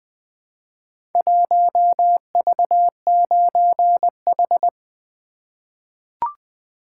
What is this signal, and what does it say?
Morse code, heard as a keyed tone, about 20 wpm, 700 Hz. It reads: 1V9H
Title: Four Letter-Number Combinations - 20wpm